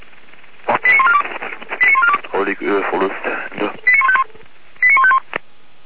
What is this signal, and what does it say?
Five-tone selective call (ZVEI), heard as quick runs of five tones, tones of about 80 ms, with speech. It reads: 89142 89142 89151 89151